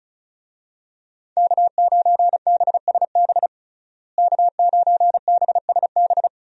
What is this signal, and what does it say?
Morse code, heard as a keyed tone, about 34 wpm, 700 Hz. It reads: K9BSB K9BSB
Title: USA Call Signs - 5 Characters - Repeated 3x - 35wpm